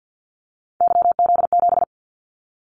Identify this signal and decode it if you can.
Morse code, heard as a keyed tone, about 50 wpm, 700 Hz. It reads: YZ7
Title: Callsign Prefixes for Top 25 DXCC - 50wpm